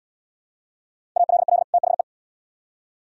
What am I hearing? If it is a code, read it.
Morse code, heard as a keyed tone, about 72 wpm, 700 Hz. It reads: ALL THE